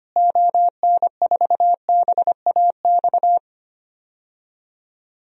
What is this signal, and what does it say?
Morse code, heard as a keyed tone, about 25 wpm, 700 Hz. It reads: ON4BAX
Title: DXCC-100 Call Signs - 6 Characters - Repeated 2x - 25wpm